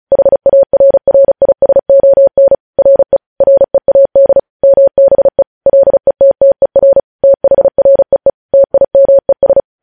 Morse code, heard as a keyed tone, about 36 wpm, 550 Hz. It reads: HARRISON RE READ MBE LETTER THREE TIMES
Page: news headlines in morse code at 35 wpm